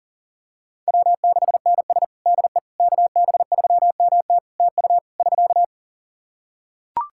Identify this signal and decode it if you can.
Morse code, heard as a keyed tone, about 40 wpm, 700 Hz. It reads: W6NS DE KB3MT TU <SK>
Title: Practice QSO - Exchange 4 - Ending - 40wpm